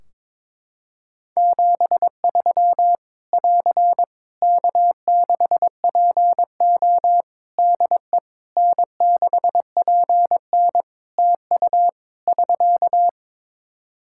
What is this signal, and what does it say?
Morse code, heard as a keyed tone, about 22 wpm, 700 Hz. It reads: 73 <AR> K6PO DE N6PN TU <SK>
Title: K1USN SST QSO Practice 22wpm